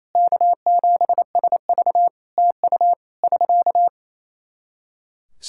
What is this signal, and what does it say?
Morse code, heard as a keyed tone, about 28 wpm, 700 Hz. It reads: K7SV TU <SK>
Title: Practice QSO - Exchange 4 - Ending - 28wpm